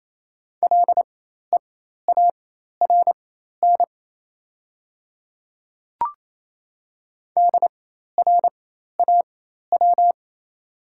Morse code, heard as a keyed tone, about 28 wpm, spaced as for 13 wpm, 700 Hz. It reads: LEARN DRAW
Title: 4x Character Spacing - Top 500 Words - 28wpm